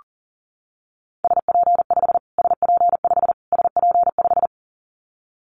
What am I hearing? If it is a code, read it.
Morse code, heard as a keyed tone, about 37 wpm, 700 Hz. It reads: SP5 SP5 SP5
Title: Callsign Prefixes for Top 25 DXCC - Repeated 3x - 40wpm